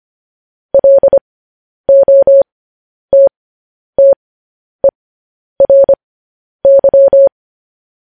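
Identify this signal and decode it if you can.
Morse code, heard as a keyed tone, about 25 wpm, spaced as for 10 wpm, 550 Hz. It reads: LOTTERY
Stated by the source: Morse code news headlines broadcast.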